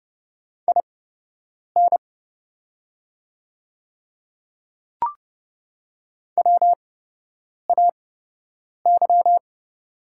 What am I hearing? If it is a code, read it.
Morse code, heard as a keyed tone, about 30 wpm, spaced as for 8 wpm, 700 Hz. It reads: IN WAY
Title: 8x Character Spacing - Top 100 Words - 30wpm